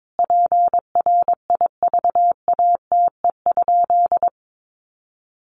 Morse code, heard as a keyed tone, about 22 wpm, 700 Hz. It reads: PRIVATE?
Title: Sentences from Top 1000 Words 22wpm